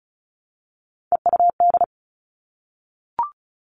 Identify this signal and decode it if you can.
Morse code, heard as a keyed tone, about 36 wpm, 700 Hz. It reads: EUD
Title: Three Challenging Letter-Number Combinations 35wpm